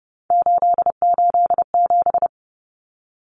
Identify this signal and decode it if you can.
Morse code, heard as a keyed tone, about 30 wpm, 700 Hz. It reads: 887
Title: Common Cut Numbers - 3-Digit 30wpm